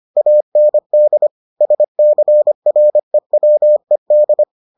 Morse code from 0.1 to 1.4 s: AND

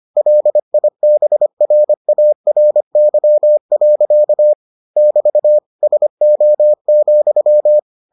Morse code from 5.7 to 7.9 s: SO,

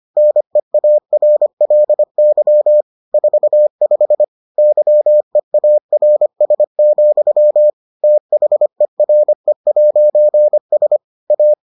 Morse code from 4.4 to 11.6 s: YEARS, THERE'S A

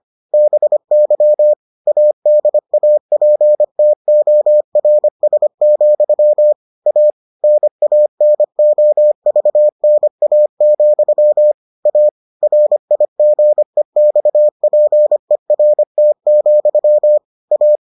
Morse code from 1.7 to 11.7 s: ADAPTORS, A NANOVNA,

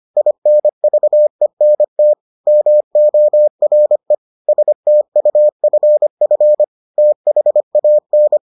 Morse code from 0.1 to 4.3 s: INVENT MORE

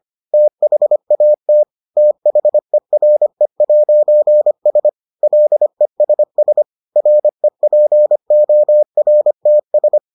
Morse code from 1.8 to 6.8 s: THERE'S LESS

